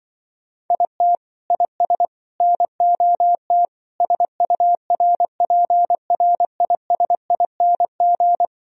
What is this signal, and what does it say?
Morse code, heard as a keyed tone, about 24 wpm, 700 Hz. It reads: IT IS NOT SURPRISING
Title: Speed-Racing - Sets of 5 Words 20wpm